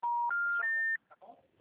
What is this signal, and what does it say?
Special information tone, with speech.